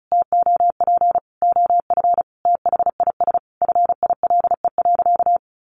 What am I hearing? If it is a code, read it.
Morse code, heard as a keyed tone, about 34 wpm, 700 Hz. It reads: TOP OF THIS FILE.
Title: Sentences from Top 1200 Words 35wpm